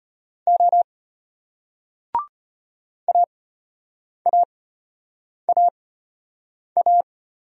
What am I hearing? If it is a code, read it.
Morse code, heard as a keyed tone, about 35 wpm, spaced as for 8 wpm, 700 Hz. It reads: O AAAA